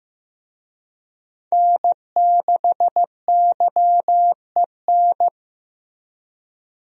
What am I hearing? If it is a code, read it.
Morse code, heard as a keyed tone, about 15 wpm, 700 Hz. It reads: N6YEN
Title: USA Call Signs - 5 Characters - Repeated 3x - 15wpm